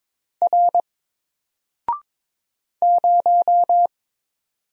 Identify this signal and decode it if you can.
Morse code, heard as a keyed tone, about 22 wpm, 700 Hz. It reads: R 0
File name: Single-Letter-Number-22wpm